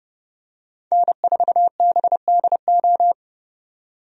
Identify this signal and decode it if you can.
Morse code, heard as a keyed tone, about 29 wpm, 700 Hz. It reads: N4BDO
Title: USA Call Signs - 5 Characters - 30wpm